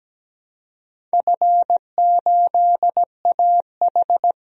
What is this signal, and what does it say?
Morse code, heard as a keyed tone, about 17 wpm, 700 Hz. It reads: F8AH
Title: DXCC-100 Call Signs - 4 Characters - 17wpm